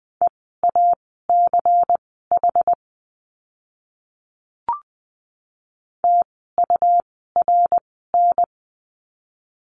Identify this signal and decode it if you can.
Morse code, heard as a keyed tone, about 20 wpm, spaced as for 14 wpm, 700 Hz. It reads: EACH TURN
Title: 192 Lesson 31f - Letter v - Words 20wpm